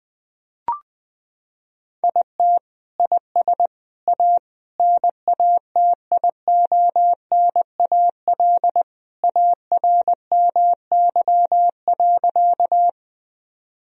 Morse code, heard as a keyed tone, about 20 wpm, 700 Hz. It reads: IT IS A NATIONAL ARMY.
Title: Sentences from Top 1000 Words - Review Part 4 - 20wpm